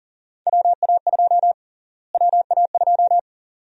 Morse code, heard as a keyed tone, about 38 wpm, 700 Hz. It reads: WA2 WA2
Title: USA Call Sign Prefixes - Repeated 2x - 40wpm